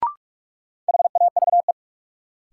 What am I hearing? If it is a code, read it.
Morse code, heard as a keyed tone, about 43 wpm, 700 Hz. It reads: HAVE
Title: Sentences from Top 100 Words - New Word Drill - 45wpm